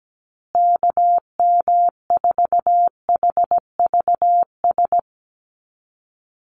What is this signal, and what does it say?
Morse code, heard as a keyed tone, about 17 wpm, 700 Hz. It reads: KM4HVS
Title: USA Call Signs Repeated 2x 17wpm